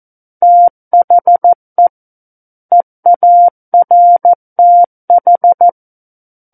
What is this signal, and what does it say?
Morse code, heard as a keyed tone, about 14 wpm, 700 Hz. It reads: THE EARTH
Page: War of the Worlds - 05-Chapter 5 - 14 WPM